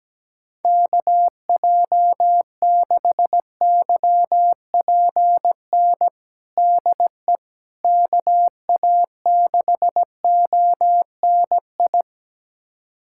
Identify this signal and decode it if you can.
Morse code, heard as a keyed tone, about 17 wpm, 700 Hz. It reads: KJ6YPN DE KA6ONI
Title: USA Callsign DE Callsign - 6 Characters - 17wpm